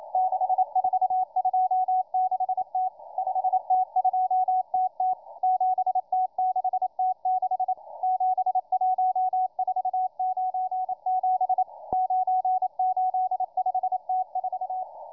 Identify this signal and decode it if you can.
Morse code, heard as a keyed tone, about 28 wpm, 740 Hz, with radio noise. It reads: 6426T 5T2TT 7T6T6 71497 985T4